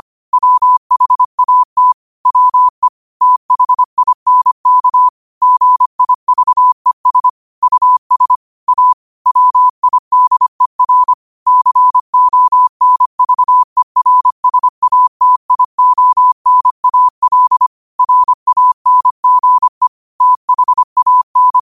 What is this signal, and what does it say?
Morse code, heard as a keyed tone, about 25 wpm, 1000 Hz. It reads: WHAT WE THINK GIVES US A WIDER CONVERSATIONAL RANGE THAN